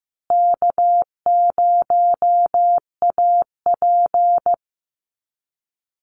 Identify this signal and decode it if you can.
Morse code, heard as a keyed tone, about 15 wpm, 700 Hz. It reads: K0AP
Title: Easy Call Signs Repeated 3x - 15wpm